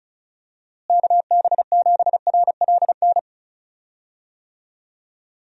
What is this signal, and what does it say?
Morse code, heard as a keyed tone, about 36 wpm, 700 Hz. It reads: KB7RLN
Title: USA Call Signs - 6 Characters - Repeated 2x - 35wpm